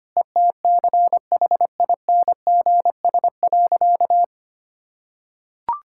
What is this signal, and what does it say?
Morse code, heard as a keyed tone, about 25 wpm, 700 Hz. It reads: ETCHINGS.
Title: Pangram Challenge v2 - 25wpm